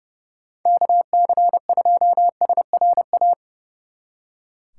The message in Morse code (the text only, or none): KC2SRA